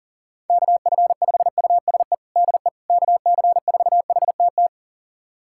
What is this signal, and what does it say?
Morse code, heard as a keyed tone, about 37 wpm, 700 Hz. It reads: KF5USE DE KC4HTT
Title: USA Callsign DE Callsign 40wpm